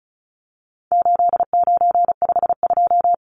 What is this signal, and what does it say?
Morse code, heard as a keyed tone, about 34 wpm, 700 Hz. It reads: 8952
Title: Numbers - 4-Digit 35wpm